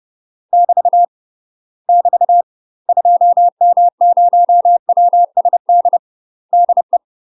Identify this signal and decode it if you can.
Morse code, heard as a keyed tone, about 30 wpm, 700 Hz. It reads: <BT> <BT> 2M0WSD DE
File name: UK MASTER QSOs TRAINING / 30WPM / UK MASTER QSOs 0032